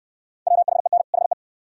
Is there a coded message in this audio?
Yes: Morse code from 0.5 to 1.3 s, heard as a keyed tone, about 66 wpm, 700 Hz. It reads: WHEN HE